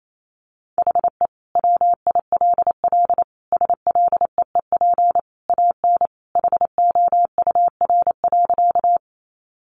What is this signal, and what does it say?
Morse code, heard as a keyed tone, about 28 wpm, 700 Hz. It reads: HE WILL SLEEP AN HOUR.
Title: Sentences from Top 500 Words 28wpm